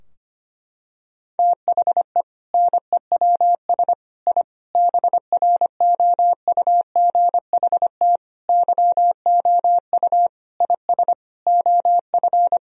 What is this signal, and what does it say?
Morse code, heard as a keyed tone, about 25 wpm, 700 Hz. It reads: THE NEWS I BROUGHT YOU IS OF
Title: Sentences from Top 1000 Words - Review Part 3 - 25wpm